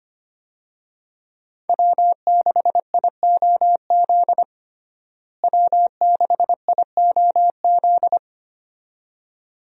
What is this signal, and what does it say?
Morse code, heard as a keyed tone, about 25 wpm, 700 Hz. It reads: W6IOZ W6IOZ
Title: USA Call Signs - 5 Characters - Repeated 2x - 25wpm